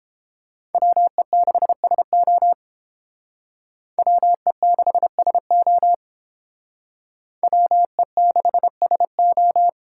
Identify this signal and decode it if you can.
Morse code, heard as a keyed tone, about 31 wpm, 700 Hz. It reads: WE6SO WE6SO WE6SO